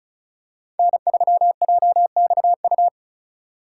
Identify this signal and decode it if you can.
Morse code, heard as a keyed tone, about 34 wpm, 700 Hz. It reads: N3JXU
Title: USA Call Signs - 3x Character Spacing - 35wpm